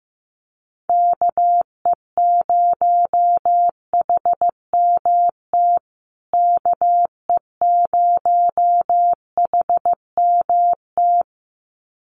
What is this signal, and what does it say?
Morse code, heard as a keyed tone, about 15 wpm, 700 Hz. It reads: KE0HMT KE0HMT
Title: USA Call Signs Repeated 2x 15wpm